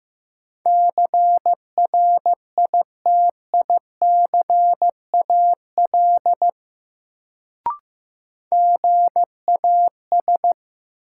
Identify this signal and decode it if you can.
Morse code, heard as a keyed tone, about 15 wpm, 700 Hz. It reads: CRITICAL GAS
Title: Sentences from Top 1100 Words - New Word Drill - 15wpm